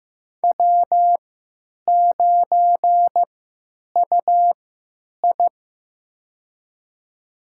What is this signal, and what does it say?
Morse code, heard as a keyed tone, about 15 wpm, spaced as for 9 wpm, 700 Hz. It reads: W9UI
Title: USA Call Signs - 3x Character Spacing - 15wpm